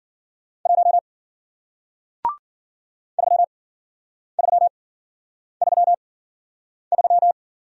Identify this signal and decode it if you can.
Morse code, heard as a keyed tone, about 49 wpm, spaced as for 9 wpm, 700 Hz. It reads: 1 3333
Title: Speed-Racing Single Letter-Number 40wpm